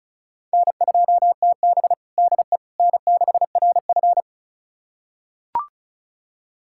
Morse code, heard as a keyed tone, about 34 wpm, 690 Hz. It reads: N2TB DE N6RF
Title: USA Callsign DE Callsign - 4 Characters - 35wpm